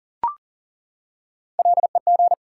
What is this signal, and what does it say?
Morse code, heard as a keyed tone, about 40 wpm, 700 Hz. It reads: LEG
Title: Three Letter Words - 40wpm